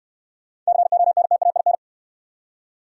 Morse code, heard as a keyed tone, about 69 wpm, 700 Hz. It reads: CONTAIN